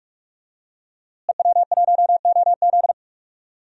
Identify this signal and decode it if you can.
Morse code, heard as a keyed tone, about 46 wpm, 700 Hz. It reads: EW1OZ
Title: DXCC-100 Call Signs - 5 Characters - Repeated 3x - 45wpm